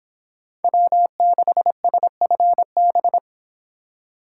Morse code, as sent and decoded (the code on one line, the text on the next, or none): .-- -.... ... ..-. -...
W6SFB